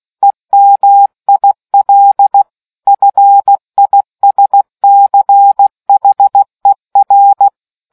Morse code, heard as a keyed tone, about 16 wpm, 800 Hz. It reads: EMIL FISCHER